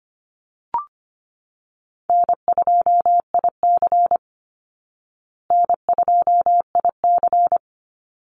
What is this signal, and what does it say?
Morse code, heard as a keyed tone, about 25 wpm, 700 Hz. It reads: N2IC N2IC
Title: Call Signs Repeated 2x - 25wpm